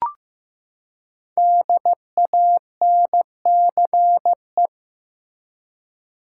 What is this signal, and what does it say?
Morse code, heard as a keyed tone, about 15 wpm, 700 Hz. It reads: DANCE
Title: Five Letter Words - 15wpm